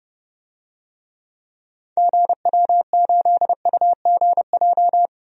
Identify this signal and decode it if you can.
Morse code, heard as a keyed tone, about 30 wpm, 700 Hz. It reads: GW8UGJ
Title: DXCC-100 Call Signs - 6 Characters - Repeated 3x - 30wpm